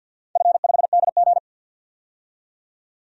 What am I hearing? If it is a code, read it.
Morse code, heard as a keyed tone, about 48 wpm, 700 Hz. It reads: F5DG
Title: DXCC-100 Call Signs - 4 Characters - 50wpm